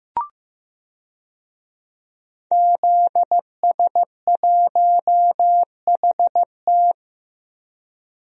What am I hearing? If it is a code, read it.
Morse code, heard as a keyed tone, about 15 wpm, 700 Hz. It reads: ZS1HT
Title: DXCC-100 Call Signs - 5 Characters - 15wpm